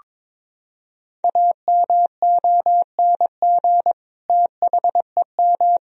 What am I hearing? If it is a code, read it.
Morse code, heard as a keyed tone, about 22 wpm, 700 Hz. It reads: AMONG THEM
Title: Sentences from Top 1000 Words - Review Part 3 - 22wpm